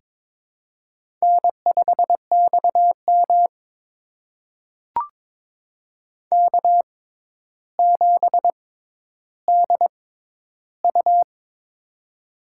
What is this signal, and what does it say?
Morse code, heard as a keyed tone, about 23 wpm, spaced as for 11 wpm, 700 Hz. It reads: N5XM K7DU
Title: USA Call Signs - 6x Character Spacing - 22wpm